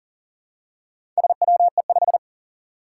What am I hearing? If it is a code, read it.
Morse code, heard as a keyed tone, about 40 wpm, 700 Hz. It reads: SWE5
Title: Four Challenging Letter-Number Combinations 40wpm